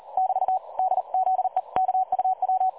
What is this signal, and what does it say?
Morse code, heard as a keyed tone, about 36 wpm, 750 Hz. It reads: <BT> DZEKUW
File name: dlazaawansowanych / nagraniazpasma / websdr recording 3565.0kHz